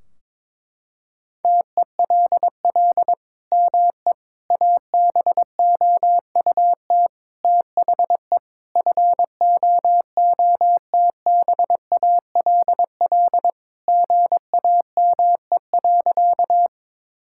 Morse code, heard as a keyed tone, about 22 wpm, 700 Hz. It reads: TELL ME ABOUT THE FOOTBALL GAME.
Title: Sentences from Top 1600 Words 22wpm